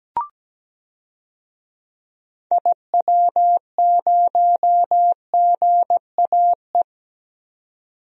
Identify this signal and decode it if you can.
Morse code, heard as a keyed tone, about 17 wpm, 700 Hz. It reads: IW0GAE
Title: DXCC-100 Call Signs - 6 Characters - 17wpm